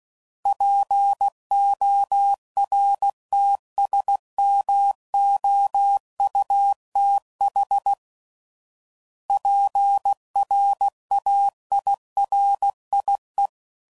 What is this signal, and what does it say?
Morse code, heard as a keyed tone, about 16 wpm, 800 Hz. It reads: PORTSMOUTH PRAIRIE
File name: CW Code Groups @15WPM